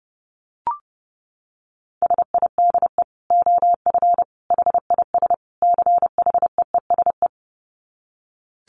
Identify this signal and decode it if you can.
Morse code, heard as a keyed tone, about 28 wpm, 700 Hz. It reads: SIDE OF HIS CHEESE
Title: Dit Challenge 30wpm